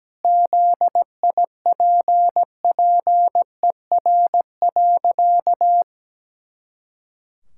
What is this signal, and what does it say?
Morse code, heard as a keyed tone, about 17 wpm, 700 Hz. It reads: ZIPPER.
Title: Pangram Challenge v2 - 17wpm